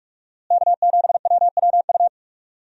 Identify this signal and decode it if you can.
Morse code, heard as a keyed tone, about 42 wpm, 700 Hz. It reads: K7WWU
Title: USA Call Signs - 5 Characters - Repeated 2x - 45wpm